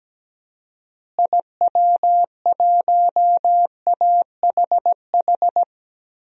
Morse code, heard as a keyed tone, about 17 wpm, 700 Hz. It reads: IW1AHH
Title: DXCC-100 Call Signs - 6 Characters - Repeated 2x - 17wpm